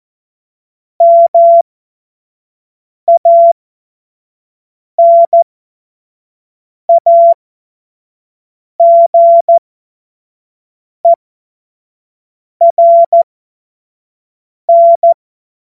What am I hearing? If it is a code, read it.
Morse code, heard as a keyed tone, about 14 wpm, spaced as for 5 wpm, 680 Hz. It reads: MANAGERN
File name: Morsekurs / Witze / Witz 22. August 2023 25 F70